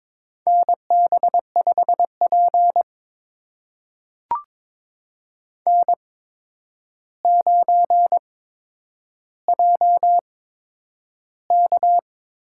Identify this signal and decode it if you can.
Morse code, heard as a keyed tone, about 21 wpm, spaced as for 9 wpm, 700 Hz. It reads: NB5P N9JK
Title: USA Call Signs - 8x Character Spacing - 22wpm